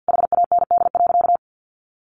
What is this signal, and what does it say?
Morse code, heard as a keyed tone, about 48 wpm, 700 Hz. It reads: HAND.
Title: Sentences from Top 1200 Words 50wpm